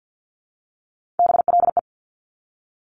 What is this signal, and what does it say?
Morse code, heard as a keyed tone, about 50 wpm, 700 Hz. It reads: BLE
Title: Top Three Letter Combinations in Words 50wpm